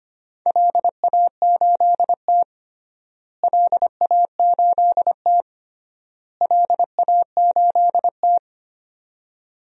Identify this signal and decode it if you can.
Morse code, heard as a keyed tone, about 25 wpm, 700 Hz. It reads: LA8T LA8T LA8T
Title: DXCC-100 Call Signs - 4 Characters - Repeated 3x - 25wpm